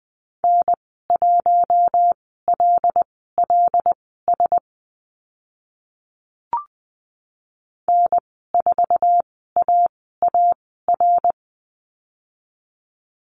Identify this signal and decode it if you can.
Morse code, heard as a keyed tone, about 20 wpm, spaced as for 14 wpm, 700 Hz. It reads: N1LLS N4AAR
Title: 076 Lesson 13g - Letter L - Call Signs 20wpm